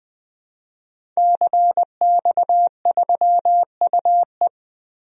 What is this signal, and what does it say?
Morse code, heard as a keyed tone, about 20 wpm, 700 Hz. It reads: CX3UE
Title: DXCC-100 Call Signs - 5 Characters - Repeated 3x - 20wpm